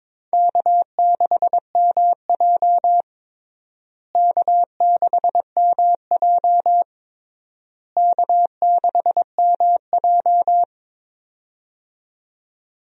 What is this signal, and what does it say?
Morse code, heard as a keyed tone, about 22 wpm, 700 Hz. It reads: K6MJ K6MJ K6MJ
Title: USA Call Signs - 4 Characters - Repeated 3x - 22wpm